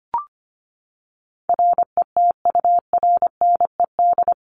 Morse code, heard as a keyed tone, about 25 wpm, 700 Hz. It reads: RETURNED
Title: ED Suffix - Top 100 Words with Base Words 25wpm